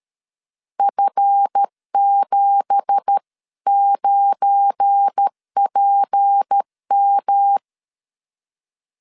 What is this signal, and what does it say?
Morse code, heard as a keyed tone, about 12 wpm, 800 Hz. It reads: F79PM
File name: LEZIONI / 10 SIGNS / 10C DOT 12wpm 2451